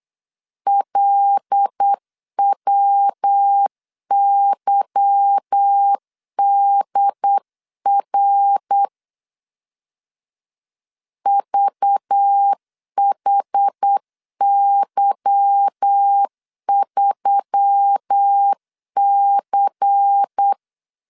Morse code, heard as a keyed tone, about 8 wpm, 800 Hz. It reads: LWYDR VHY3C
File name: CW / DET / 1 9 1500C S C 8wpm 5653